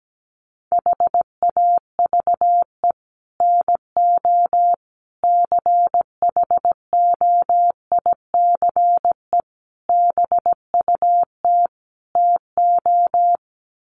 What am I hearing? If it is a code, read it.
Morse code, heard as a keyed tone, about 17 wpm, 700 Hz. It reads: HAVE NO CHOICE BUT TO
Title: Sets of 5 Words Encore 17wpm